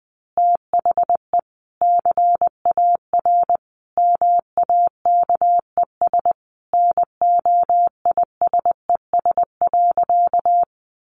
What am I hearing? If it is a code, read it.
Morse code, heard as a keyed tone, about 20 wpm, 700 Hz. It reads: THE CAR MAKES NOISES.
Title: Easy Phrases 20wpm